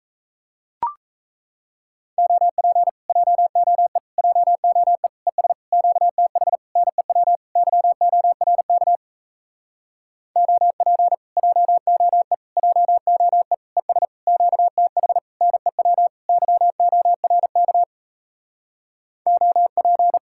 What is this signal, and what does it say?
Morse code, heard as a keyed tone, about 38 wpm, 700 Hz. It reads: OP JOE JOE ES QTH NEW YORK OP JOE JOE ES QTH NEW YORK OP